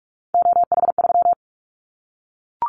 Morse code, heard as a keyed tone, about 44 wpm, 700 Hz. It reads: OH3
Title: Callsign Prefixes for Top 25 DXCC - 45wpm